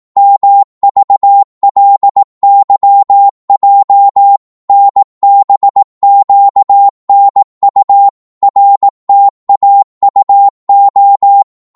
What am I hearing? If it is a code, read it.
Morse code, heard as a keyed tone, about 18 wpm, 800 Hz. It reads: MVLYJ NBQNU RTAUO
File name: Morse / ltr-18wpm